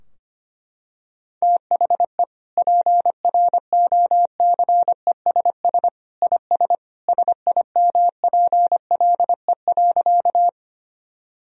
Morse code, heard as a keyed tone, about 25 wpm, 700 Hz. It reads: THE PROCESS IS SIMPLE.
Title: Sentences from Top 1000 Words - Review Part 3 - 25wpm